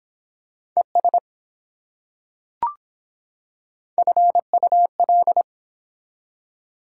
Morse code, heard as a keyed tone, about 23 wpm, 700 Hz. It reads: ES FUL